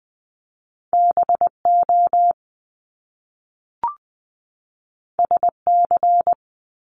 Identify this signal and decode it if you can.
Morse code, heard as a keyed tone, about 20 wpm, 700 Hz. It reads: B O S C
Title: Top Two Letter Combinations in Words 20wpm